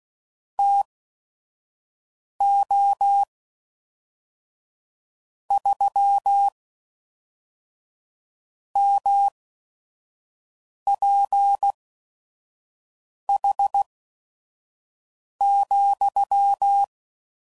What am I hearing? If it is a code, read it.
Morse code, heard as a keyed tone, about 16 wpm, spaced as for 5 wpm, 800 Hz. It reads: TO 3 MPH,
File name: CW Code Groups @5WPM / Weather Sets / Weather 5 1